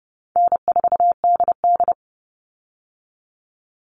Morse code, heard as a keyed tone, about 30 wpm, 700 Hz. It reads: N4DD
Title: Easy Call Signs - 30wpm